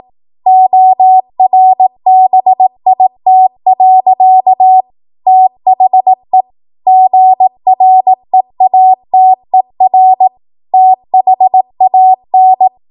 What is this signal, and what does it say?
Morse code, heard as a keyed tone, about 18 wpm, 750 Hz. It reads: ORBIT. THE GREATER THAN
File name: Morse / 18WPM